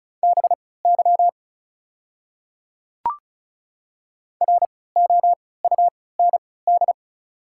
Morse code, heard as a keyed tone, about 36 wpm, spaced as for 20 wpm, 700 Hz. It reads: BY ROUND
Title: 3x Character Spacing - Top 500 Words - 35wpm